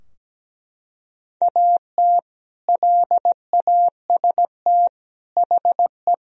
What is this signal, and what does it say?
Morse code, heard as a keyed tone, about 17 wpm, 700 Hz. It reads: AT LAST HE
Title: Sentences from Top 1000 Words - Review Part 4 - 17wpm